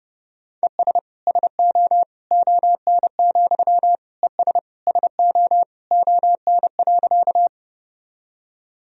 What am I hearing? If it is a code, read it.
Morse code, heard as a keyed tone, about 30 wpm, 700 Hz. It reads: ES SO ON, ES SO ON.
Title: Sentences from Top 100 Words and Abbreviations 30wpm